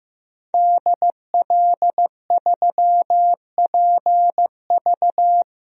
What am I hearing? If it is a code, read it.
Morse code, heard as a keyed tone, about 15 wpm, 700 Hz. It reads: DL3PV
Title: DXCC-100 Call Signs - 5 Characters - Repeated 2x - 15wpm